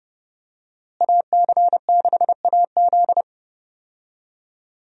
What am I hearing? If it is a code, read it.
Morse code, heard as a keyed tone, about 30 wpm, 700 Hz. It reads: AC6AZ
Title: USA Call Signs - 5 Characters - Repeated 3x - 30wpm